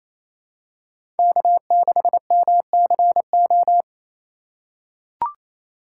Morse code, heard as a keyed tone, about 28 wpm, 700 Hz. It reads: K6MCO
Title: USA Call Signs - 5 Characters - 2x Character Spacing - 28wpm